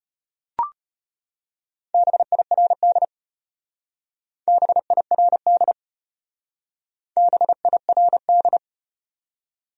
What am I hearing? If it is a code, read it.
Morse code, heard as a keyed tone, about 33 wpm, 700 Hz. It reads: BIRD BIRD BIRD